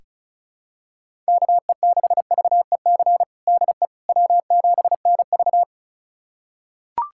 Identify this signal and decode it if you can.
Morse code, heard as a keyed tone, about 34 wpm, 700 Hz. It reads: KE6VEC DE W7NV
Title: USA Callsign DE Callsign 35wpm